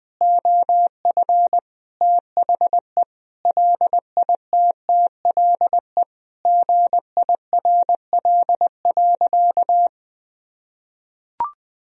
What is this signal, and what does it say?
Morse code, heard as a keyed tone, about 20 wpm, 700 Hz. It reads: OF THE LITTLE GIRL.
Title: Sentences from Top 400 Words and Abbreviations 20wpm